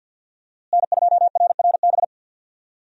Morse code, heard as a keyed tone, about 46 wpm, 700 Hz. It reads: N2RRB
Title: USA Call Signs - 5 Characters - Repeated 3x - 50wpm